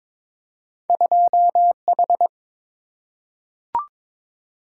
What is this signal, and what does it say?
Morse code, heard as a keyed tone, about 22 wpm, 700 Hz. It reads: 2H
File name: practice / 22wpm / Two Letter-Number Combinations - 22wpm